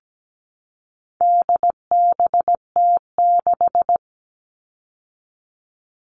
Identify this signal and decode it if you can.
Morse code, heard as a keyed tone, about 17 wpm, 700 Hz. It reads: DBT6
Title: All Cut Numbers - 4-Digit 17wpm